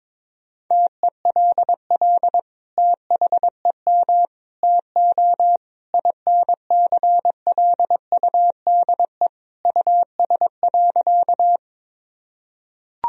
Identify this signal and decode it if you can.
Morse code, heard as a keyed tone, about 21 wpm, 700 Hz. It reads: TELL THEM TO INCLUDE US.
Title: Sentences from Top 1000 Words - Review Part 3 - 22wpm